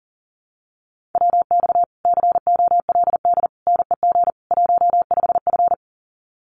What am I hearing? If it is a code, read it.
Morse code, heard as a keyed tone, about 38 wpm, 700 Hz. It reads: WX COLD NEG 15F
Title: Common QSO Phrases Encore 40wpm